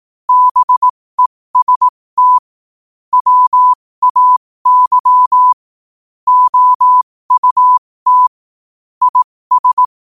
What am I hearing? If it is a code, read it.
Morse code, heard as a keyed tone, about 18 wpm, spaced as for 15 wpm, 1000 Hz. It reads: BEST WAY OUT IS
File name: qotdfiles / 15wpm